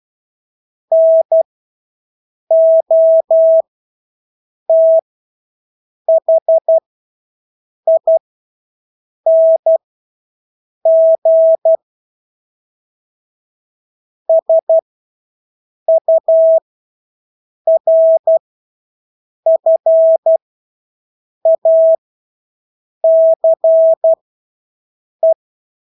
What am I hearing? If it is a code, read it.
Morse code, heard as a keyed tone, about 12 wpm, spaced as for 6 wpm, 650 Hz. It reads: NOTHING SURFACE
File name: Engelske ord 12-6 wpm